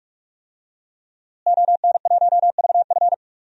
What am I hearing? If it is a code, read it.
Morse code, heard as a keyed tone, about 46 wpm, 700 Hz. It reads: ON1VF
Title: DXCC-100 Call Signs - 5 Characters - Repeated 3x - 45wpm